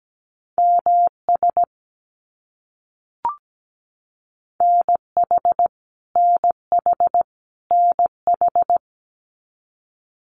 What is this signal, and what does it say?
Morse code, heard as a keyed tone, about 17 wpm, 700 Hz. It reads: MS NH NH NH
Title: US States Repeated 3x 17wpm